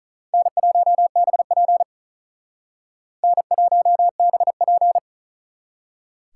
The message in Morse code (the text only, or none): N1BP N1BP